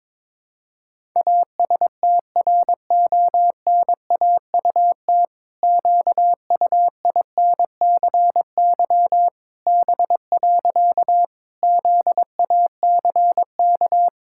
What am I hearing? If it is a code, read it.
Morse code, heard as a keyed tone, about 22 wpm, 700 Hz. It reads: ASTRONAUT QUINCY B. ZACK
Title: Pangram Challenge v2 - 22wpm